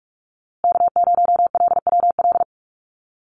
Morse code, heard as a keyed tone, about 44 wpm, 700 Hz. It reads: K0LWL
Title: USA Call Signs Repeated 3x 45wpm